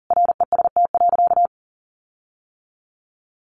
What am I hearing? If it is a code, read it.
Morse code, heard as a keyed tone, about 40 wpm, 700 Hz. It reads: REST.
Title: Sentences from Top 1300 Words 40wpm